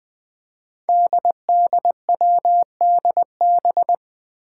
Morse code, heard as a keyed tone, about 20 wpm, 700 Hz. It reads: DDWDB
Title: Five Challenging Letter-Number Combinations 20wpm